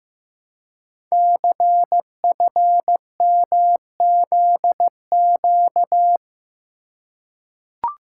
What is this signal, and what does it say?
Morse code, heard as a keyed tone, about 15 wpm, 700 Hz. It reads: CFMZQ
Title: Code Groups - 5 Letters - 15wpm